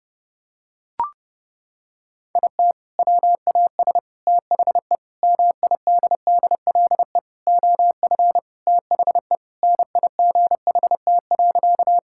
Morse code, heard as a keyed tone, about 32 wpm, 700 Hz. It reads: IT WAS THE MIDDLE OF THE NIGHT.
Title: Sentences from Top 1000 Words - Review Part 6 - 30wpm